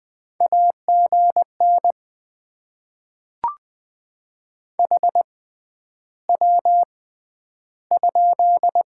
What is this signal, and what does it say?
Morse code, heard as a keyed tone, about 20 wpm, 700 Hz. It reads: A G N H W ?
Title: Top 20 QSO Elements - 6x Character Spacing - 20wpm